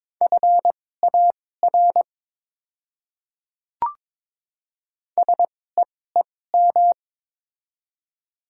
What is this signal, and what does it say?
Morse code, heard as a keyed tone, about 22 wpm, spaced as for 16 wpm, 700 Hz. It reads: FAR SEEM
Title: 2x Character Spacing - Top 300 Words - 22wpm